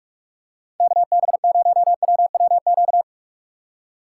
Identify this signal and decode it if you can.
Morse code, heard as a keyed tone, about 45 wpm, 700 Hz. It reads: KB0WWQ